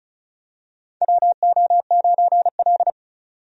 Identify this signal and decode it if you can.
Morse code, heard as a keyed tone, about 34 wpm, 700 Hz. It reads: WO9L
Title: USA Call Signs - 4 Characters - 35wpm